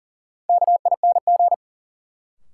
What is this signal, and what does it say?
Morse code, heard as a keyed tone, about 40 wpm, 700 Hz. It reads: KING
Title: Sentences from Top 400 Words - New Word Drill - 40wpm